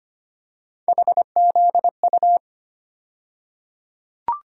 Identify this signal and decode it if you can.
Morse code, heard as a keyed tone, about 25 wpm, 700 Hz. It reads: HZU
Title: USA Call Sign Postfixes - 3 Letters 25wpm